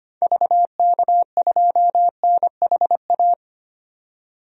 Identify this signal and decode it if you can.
Morse code, heard as a keyed tone, about 25 wpm, 700 Hz. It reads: VK2NHA
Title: DXCC-100 Call Signs - 6 Characters - 25wpm